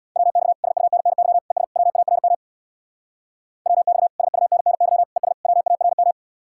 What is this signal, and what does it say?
Morse code, heard as a keyed tone, about 64 wpm, 700 Hz. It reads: WX SUNNY ES WARM WX SUNNY ES WARM